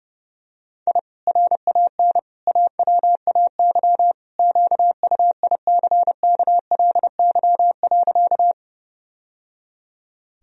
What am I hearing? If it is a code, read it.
Morse code, heard as a keyed tone, about 30 wpm, 700 Hz. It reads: I RAN AWAY QUICKLY.
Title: Sentences from Top 1000 Words - Review Part 3 - 30wpm